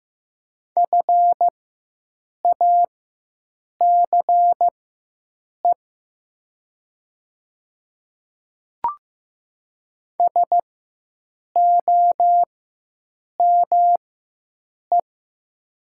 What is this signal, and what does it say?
Morse code, heard as a keyed tone, about 15 wpm, spaced as for 7 wpm, 700 Hz. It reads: FACE SOME
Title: 4x Character Spacing - Top 500 Words - 15wpm